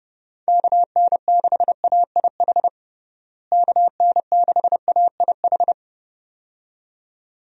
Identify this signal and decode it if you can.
Morse code, heard as a keyed tone, about 30 wpm, 700 Hz. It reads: KN6AIH KN6AIH
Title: USA Call Signs - 6 Characters - Repeated 2x - 30wpm